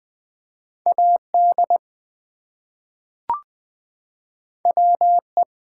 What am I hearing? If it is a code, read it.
Morse code, heard as a keyed tone, about 20 wpm, 700 Hz. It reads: AD WE
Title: Two Letter Words - 20wpm